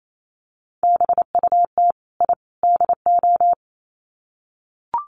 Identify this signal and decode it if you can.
Morse code, heard as a keyed tone, about 28 wpm, 700 Hz. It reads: BUT I DO
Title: Sets of 3 Words 28wpm